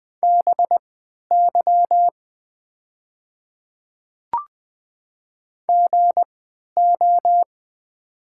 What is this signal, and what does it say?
Morse code, heard as a keyed tone, about 20 wpm, spaced as for 11 wpm, 700 Hz. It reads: BY GO
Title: 3x Character Spacing - Top 200 Words - 20wpm